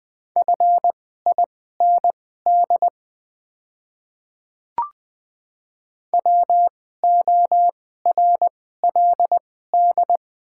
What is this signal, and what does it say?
Morse code, heard as a keyed tone, about 20 wpm, spaced as for 14 wpm, 700 Hz. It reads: FIND WORLD
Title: 2x Character Spacing - Top 300 Words - 20wpm